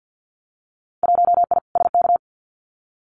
Morse code, heard as a keyed tone, about 50 wpm, 700 Hz. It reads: 1I SK
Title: Winter Field Day Exchanges Encore 50wpm